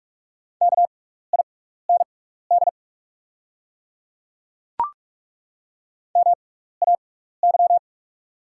Morse code, heard as a keyed tone, about 43 wpm, spaced as for 15 wpm, 700 Hz. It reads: KIND MAY
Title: 6x Character Spacing - Top 200 Words - 45wpm